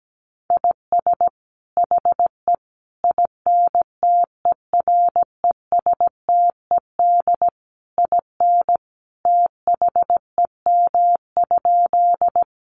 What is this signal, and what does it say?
Morse code, heard as a keyed tone, about 17 wpm, 700 Hz. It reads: IS HE INTERESTED IN THEM?
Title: Sentences from Top 900 Words 17wpm